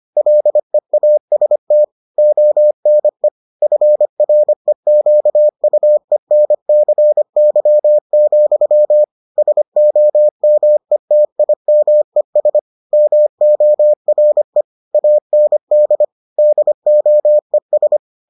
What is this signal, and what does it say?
Morse code, heard as a keyed tone, about 25 wpm, 600 Hz. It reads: LEAST ONE FREQUENCY, SOMETIMES MORE AND DOES